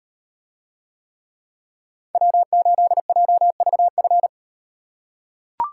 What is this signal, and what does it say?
Morse code, heard as a keyed tone, about 38 wpm, 700 Hz. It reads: W8JVF